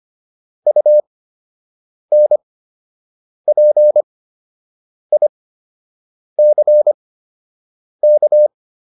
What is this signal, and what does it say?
Morse code, heard as a keyed tone, about 25 wpm, spaced as for 7 wpm, 600 Hz. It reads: UNPICK